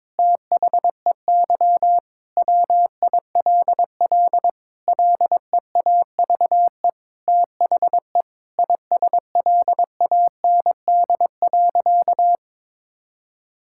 Morse code, heard as a keyed tone, about 22 wpm, 700 Hz. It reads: THEY WILL LEAVE THE ISLAND.
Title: Sentences from Top 1000 Words - Review Part 6 - 22wpm